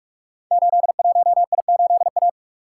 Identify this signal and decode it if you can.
Morse code, heard as a keyed tone, about 46 wpm, 700 Hz. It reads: 81I8A
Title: Code Groups - 5 Letter:Numbers - 45wpm